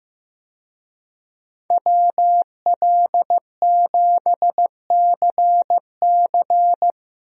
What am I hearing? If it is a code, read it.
Morse code, heard as a keyed tone, about 15 wpm, 700 Hz. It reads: WL7CC
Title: USA Call Signs - 5 Characters - 15wpm